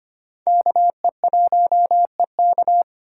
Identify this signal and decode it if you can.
Morse code, heard as a keyed tone, about 25 wpm, 700 Hz. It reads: KE1EK
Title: USA Call Signs - 5 Characters - 2x Character Spacing - 25wpm